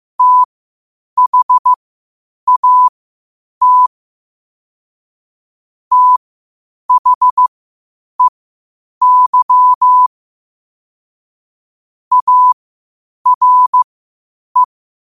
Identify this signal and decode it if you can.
Morse code, heard as a keyed tone, about 15 wpm, spaced as for 8 wpm, 1000 Hz. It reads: THAT THEY ARE
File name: qotdfiles / 8wpm